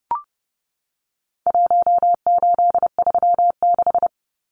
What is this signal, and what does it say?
Morse code, heard as a keyed tone, about 30 wpm, 700 Hz. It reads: 1836
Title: Numbers - Contest Style - 4-Digit 30wpm